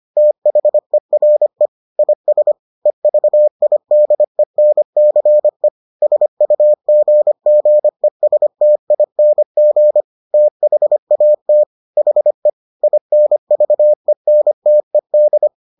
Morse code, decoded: THERE IS EVIDENCE SUGGESTING THAT HE INVENTED

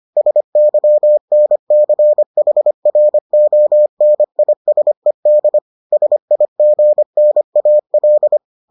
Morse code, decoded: SYNCHRONISED SIGNAL

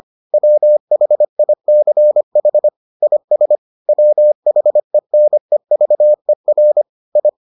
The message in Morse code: WHICH IS WHENEVER I

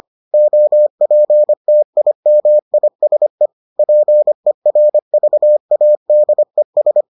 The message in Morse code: OPTIMISE PERVADES

